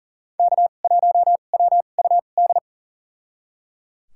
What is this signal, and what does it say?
Morse code, heard as a keyed tone, about 37 wpm, spaced as for 28 wpm, 700 Hz. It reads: K1WUD
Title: USA Call Signs - 2x Character Spacing - 40wpm